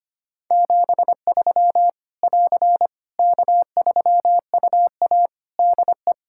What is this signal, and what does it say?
Morse code, heard as a keyed tone, about 25 wpm, 700 Hz. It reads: 73 <AR> K3UA DE
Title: Practice QSO - Exchange 4 - Ending - 25wpm